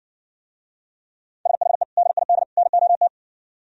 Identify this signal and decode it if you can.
Morse code, heard as a keyed tone, about 65 wpm, 700 Hz. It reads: SHE DID NOT